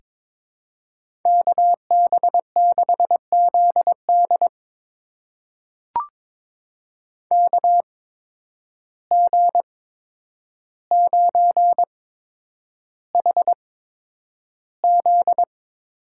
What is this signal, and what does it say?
Morse code, heard as a keyed tone, about 22 wpm, spaced as for 9 wpm, 700 Hz. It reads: KB6ZD KG9HZ
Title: USA Call Signs - 5 Characters - 8x Character Spacing - 22wpm